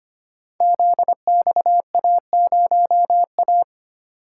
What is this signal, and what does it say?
Morse code, heard as a keyed tone, about 25 wpm, 700 Hz. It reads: ZXA0A
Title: Code Groups - 5 Letter:Numbers - 25wpm